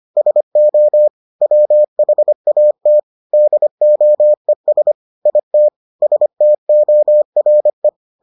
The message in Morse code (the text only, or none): SO WHAT DOES IT STORE